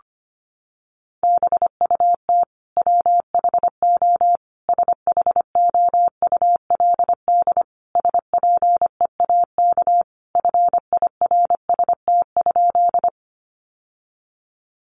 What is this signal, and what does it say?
Morse code, heard as a keyed tone, about 25 wpm, 700 Hz. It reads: BUT WHO SHOULD SPEAK FIRST?
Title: Sentences from Top 1100 Words 25wpm